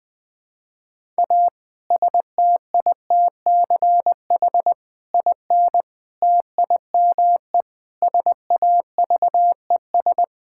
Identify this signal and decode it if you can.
Morse code, heard as a keyed tone, about 20 wpm, 700 Hz. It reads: A STITCH IN TIME SAVES
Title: Difficult Sentences with Hints - 20wpm